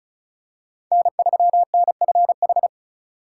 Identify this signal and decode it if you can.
Morse code, heard as a keyed tone, about 36 wpm, 700 Hz. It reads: N3NFH
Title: USA Call Signs - 5 Characters - 4x Character Spacing - 35wpm